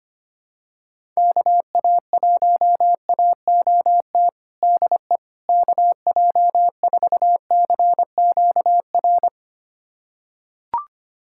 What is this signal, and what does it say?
Morse code, heard as a keyed tone, about 25 wpm, 700 Hz. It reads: KA1AOT DE KJ4CQR
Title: USA Callsign DE Callsign 25wpm